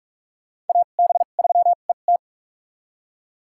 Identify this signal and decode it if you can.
Morse code, heard as a keyed tone, about 41 wpm, spaced as for 31 wpm, 690 Hz. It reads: AB3ET
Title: USA Call Signs - 5 Characters - 2x Character Spacing - 45wpm